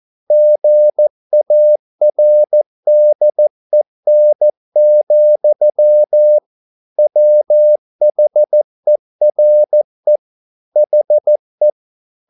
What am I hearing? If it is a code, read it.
Morse code, heard as a keyed tone, about 14 wpm, 600 Hz. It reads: GARDEN, WHERE HE
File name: mp3 files / a dogs mistake - 14wpm